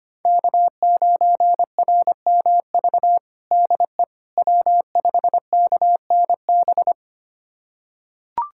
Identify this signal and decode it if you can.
Morse code, heard as a keyed tone, about 25 wpm, 700 Hz. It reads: K9RMV DE W5KNB
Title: USA Callsign DE Callsign - 5 Characters - 25wpm